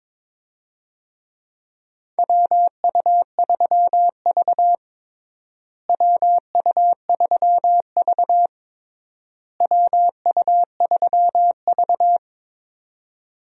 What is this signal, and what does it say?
Morse code, heard as a keyed tone, about 22 wpm, 700 Hz. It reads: WU3V WU3V WU3V
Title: USA Call Signs - 4 Characters - Repeated 3x - 22wpm